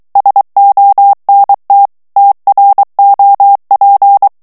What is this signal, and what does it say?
Morse code, heard as a keyed tone, about 23 wpm, 800 Hz. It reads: SONT TROP
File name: morse mp3 / FRANCAIS TEXTE CLAIR 2 VITESSE 22 MOTS MINUTES